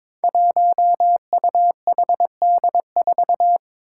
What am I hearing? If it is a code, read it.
Morse code, heard as a keyed tone, about 22 wpm, 700 Hz. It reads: 1UHD4
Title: Five Challenging Letter-Number Combinations 22wpm